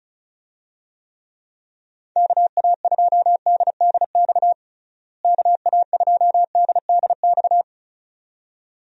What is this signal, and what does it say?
Morse code, heard as a keyed tone, about 36 wpm, 700 Hz. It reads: KA2DDX KA2DDX
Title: USA Call Signs - 6 Characters - Repeated 2x - 35wpm